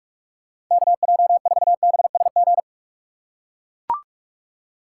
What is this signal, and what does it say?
Morse code, heard as a keyed tone, about 44 wpm, 700 Hz. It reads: KJ4BSG
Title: USA Call Signs - 6 Characters - 4x Character Spacing - 45wpm